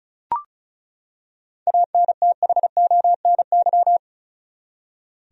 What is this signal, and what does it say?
Morse code, heard as a keyed tone, about 36 wpm, 700 Hz. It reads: ANTHONY
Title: USA Common Names - 35wpm